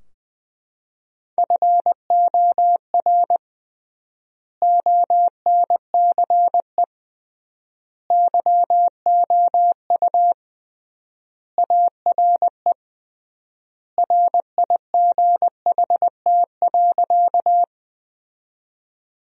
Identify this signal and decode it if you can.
Morse code, heard as a keyed tone, about 20 wpm, 700 Hz. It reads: FOR ONCE YOU ARE RIGHT.
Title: 3x Word-Spacing - Sentences from Top 300 Words - 20wpm